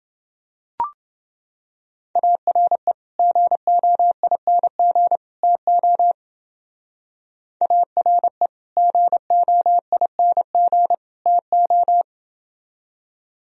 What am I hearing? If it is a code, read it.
Morse code, heard as a keyed tone, about 29 wpm, 700 Hz. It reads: ARE GOING TO ARE GOING TO